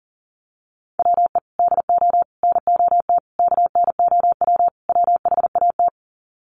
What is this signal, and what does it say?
Morse code, heard as a keyed tone, about 40 wpm, 700 Hz. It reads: WE DO NOT KNOW WHAT
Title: Sets of 5 Words 40wpm